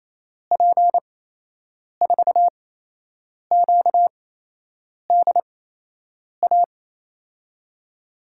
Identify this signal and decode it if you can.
Morse code, heard as a keyed tone, about 28 wpm, spaced as for 8 wpm, 700 Hz. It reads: P4QDA
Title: USA Call Signs - 6 Characters - 8x Character Spacing - 28wpm